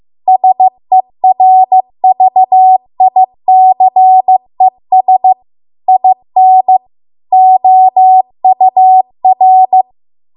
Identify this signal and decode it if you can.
Morse code, heard as a keyed tone, about 15 wpm, 750 Hz. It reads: SERVICES IN OUR